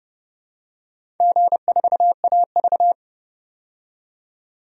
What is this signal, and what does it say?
Morse code, heard as a keyed tone, about 30 wpm, 700 Hz. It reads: G4AV
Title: DXCC-100 Call Signs - 4 Characters - 30wpm